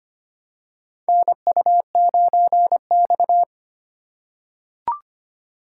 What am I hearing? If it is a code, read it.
Morse code, heard as a keyed tone, about 25 wpm, 700 Hz. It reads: NU9X
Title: USA Call Signs - 4 Characters - 2x Character Spacing - 25wpm